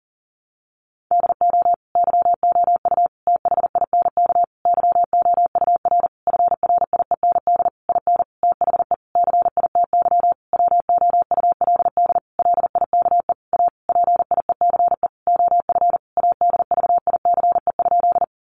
Morse code, heard as a keyed tone, about 40 wpm, 700 Hz. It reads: DO YOU THINK YOUR FRIEND IN THE CITY WOULD LIKE A PIECE OF ADVICE?
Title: Sentences from Top 1100 Words 40wpm